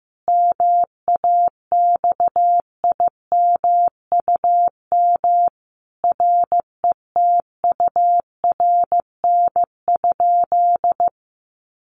Morse code, heard as a keyed tone, about 15 wpm, 700 Hz. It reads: MAXIMUM RETURN?
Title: Sentences from Top 1300 Words 15wpm